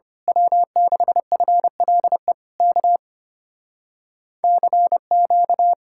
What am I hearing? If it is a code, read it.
Morse code, heard as a keyed tone, about 24 wpm, 700 Hz. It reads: W6FLE K CQ
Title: Speed-Racing Common QSO Phrases Encore - 25wpm